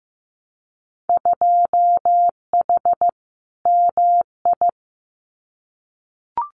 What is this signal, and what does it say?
Morse code, heard as a keyed tone, about 15 wpm, 700 Hz. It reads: 2H MI
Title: Winter Field Day Exchanges Encore 15wpm